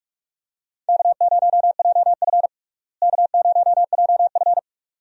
Morse code, heard as a keyed tone, about 46 wpm, 700 Hz. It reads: K0JF K0JF
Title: USA Call Signs - 4 Characters - Repeated 2x - 45wpm